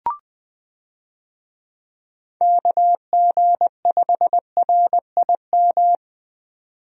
Morse code, heard as a keyed tone, about 20 wpm, 700 Hz. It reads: KG5RIM
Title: USA Call Signs - 6 Characters - 20wpm